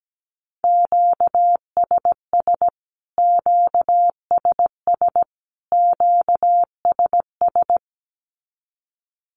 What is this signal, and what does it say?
Morse code, heard as a keyed tone, about 17 wpm, 700 Hz. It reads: QSS QSS QSS
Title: Q-Codes Repeated 3x 17wpm